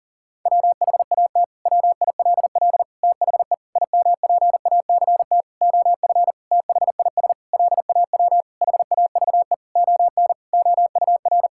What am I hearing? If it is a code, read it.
Morse code, heard as a keyed tone, about 40 wpm, 700 Hz. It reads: WHAT WILL THE IMPACT OF THIS LAW HAVE ON OUR
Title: Sentences from Top 1000 Words - Review Part 6 - 40wpm